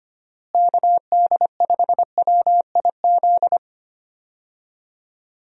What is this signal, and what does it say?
Morse code, heard as a keyed tone, about 25 wpm, 700 Hz. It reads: KD5WIZ
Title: USA Call Signs - 6 Characters - Repeated 2x - 25wpm